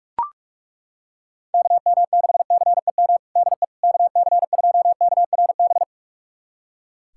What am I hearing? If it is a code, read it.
Morse code, heard as a keyed tone, about 42 wpm, 700 Hz. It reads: KM6CEM DE KC2KRB
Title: USA Callsign DE Callsign - 6 Characters - 45wpm